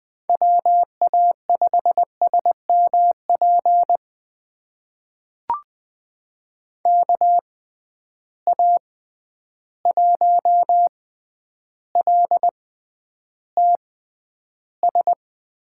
Morse code, heard as a keyed tone, about 20 wpm, spaced as for 10 wpm, 700 Hz. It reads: WA5SMP KA1LTS
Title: USA Call Signs - 6 Characters - 6x Character Spacing - 20wpm